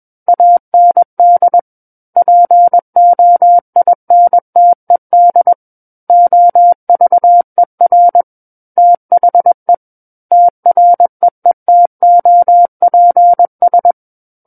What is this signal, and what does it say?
Morse code, heard as a keyed tone, about 21 wpm, 700 Hz. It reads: AND POINTED OVER THE TREETOPS
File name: Book One / 12-Chapter 12 - 21 WPM